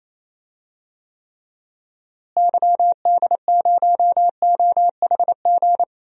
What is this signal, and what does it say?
Morse code, heard as a keyed tone, about 27 wpm, 700 Hz. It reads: YD0OHG
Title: DXCC-100 Call Signs - 6 Characters - Repeated 2x - 28wpm